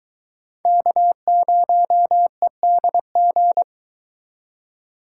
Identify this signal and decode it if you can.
Morse code, heard as a keyed tone, about 22 wpm, 700 Hz. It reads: K0EDG